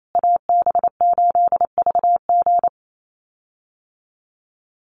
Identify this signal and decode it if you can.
Morse code, heard as a keyed tone, about 28 wpm, 700 Hz. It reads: AB8VG
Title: USA Call Signs Repeated 3x 28wpm